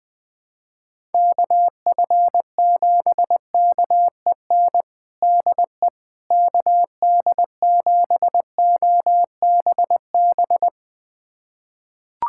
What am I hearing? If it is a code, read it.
Morse code, heard as a keyed tone, about 20 wpm, 700 Hz. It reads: KF7KEN DE KD7OBB
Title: USA Callsign DE Callsign - 6 Characters - 20wpm